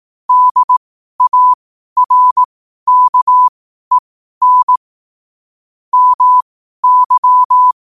Morse code, heard as a keyed tone, about 18 wpm, spaced as for 12 wpm, 1000 Hz. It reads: DARKEN MY
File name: qotdfiles / 12wpm